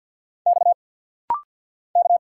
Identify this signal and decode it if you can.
Morse code, heard as a keyed tone, about 50 wpm, 700 Hz. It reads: X K
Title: Single Letter-Number Alt - Rapid-Fire 50wpm